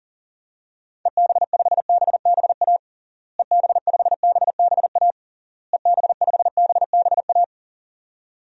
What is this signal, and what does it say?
Morse code, heard as a keyed tone, about 40 wpm, 700 Hz. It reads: EB5BBA EB5BBA EB5BBA
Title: DXCC-100 Call Signs - 6 Characters - Repeated 3x - 40wpm